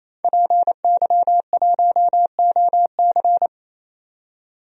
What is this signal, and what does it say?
Morse code, heard as a keyed tone, about 28 wpm, 700 Hz. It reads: PY1OC
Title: DXCC-100 Call Signs - 5 Characters - 28wpm